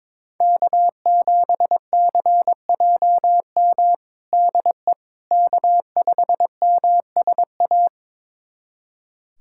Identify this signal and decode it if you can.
Morse code, heard as a keyed tone, about 22 wpm, 700 Hz. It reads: K7CJM DE K5MSA
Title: USA Callsign DE Callsign - 5 Characters - 22wpm